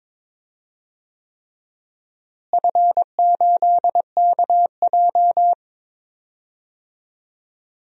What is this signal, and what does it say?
Morse code, heard as a keyed tone, about 22 wpm, 700 Hz. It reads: F8KJ
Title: DXCC-100 Call Signs - 4 Characters - Repeated 3x - 22wpm